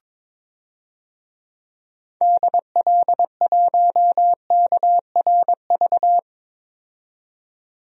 Morse code, heard as a keyed tone, about 22 wpm, 700 Hz. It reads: DL1KRV
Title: DXCC-100 Call Signs - 6 Characters - 22wpm